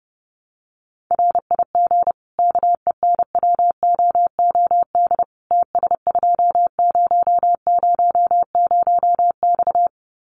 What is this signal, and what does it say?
Morse code, heard as a keyed tone, about 30 wpm, 700 Hz. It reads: RIG KENWOOD TS2000X
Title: Common QSO Phrases 30wpm